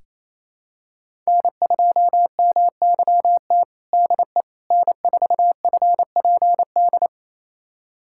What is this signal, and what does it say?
Morse code, heard as a keyed tone, about 28 wpm, 700 Hz. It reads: N2MYT DE N4FPD
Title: USA Callsign DE Callsign - 5 Characters - 28wpm